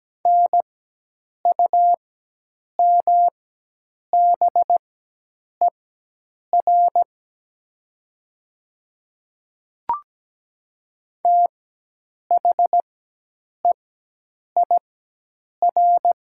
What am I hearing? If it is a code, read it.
Morse code, heard as a keyed tone, about 17 wpm, spaced as for 8 wpm, 700 Hz. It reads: NUMBER THEIR
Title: 4x Character Spacing - Top 100 Words - 17wpm